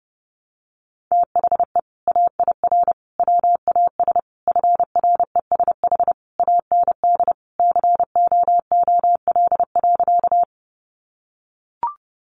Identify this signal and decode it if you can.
Morse code, heard as a keyed tone, about 30 wpm, 700 Hz. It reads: THE AIR WAS FRESH AND COOL.
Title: Sentences from Top 1200 Words 30wpm